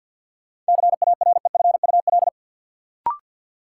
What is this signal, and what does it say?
Morse code, heard as a keyed tone, about 53 wpm, 700 Hz. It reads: CAREFUL
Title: Sentences from Top 1600 Words - New Word Drill - 50wpm